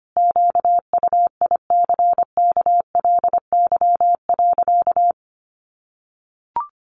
Morse code, heard as a keyed tone, about 25 wpm, 700 Hz. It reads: QUICKLY.
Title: Sentences from Top 1300 Words 25wpm